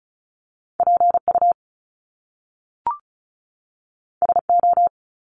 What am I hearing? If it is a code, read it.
Morse code, heard as a keyed tone, about 36 wpm, 700 Hz. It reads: PU SO
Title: Top Two Letter Combinations in Words 35wpm